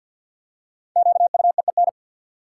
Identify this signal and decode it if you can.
Morse code, heard as a keyed tone, about 53 wpm, 700 Hz. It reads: QUEEN
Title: Five Letter Words - 50wpm